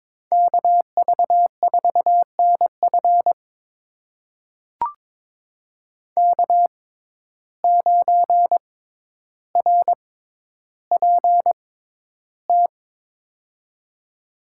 Morse code, heard as a keyed tone, about 22 wpm, spaced as for 11 wpm, 700 Hz. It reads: KV4NF K9RPT
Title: USA Call Signs - 5 Characters - 6x Character Spacing - 22wpm